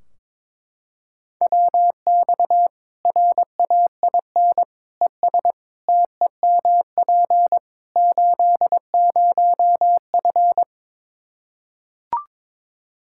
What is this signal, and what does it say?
Morse code, heard as a keyed tone, about 21 wpm, 700 Hz. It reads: WX RAIN ES TEMP 80F
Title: Practice QSO - Exchange 2 - Info - 22wpm